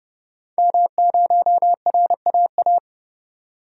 Morse code, heard as a keyed tone, about 30 wpm, 690 Hz. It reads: M0RAA
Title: DXCC-100 Call Signs - 5 Characters - 30wpm